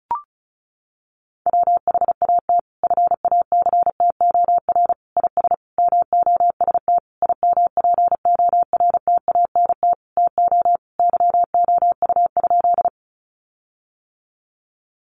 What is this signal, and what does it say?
Morse code, heard as a keyed tone, about 34 wpm, 700 Hz. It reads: WHAT FACTOR IS MOST IMPORTANT TO YOU?
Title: Sentences from Top 1300 Words 35wpm